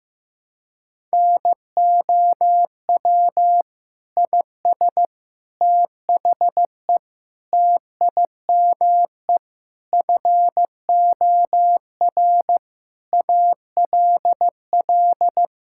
Morse code, heard as a keyed tone, about 15 wpm, 700 Hz. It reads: NOW IS THE TIME FOR ALL
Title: Pangram Challenge v2 - 15wpm